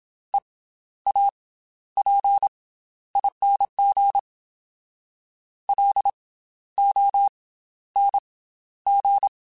Morse code, heard as a keyed tone, about 27 wpm, 800 Hz. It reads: E A P ING L O N G
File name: Special INGs / Special INGs-mp3s / Slow ing @25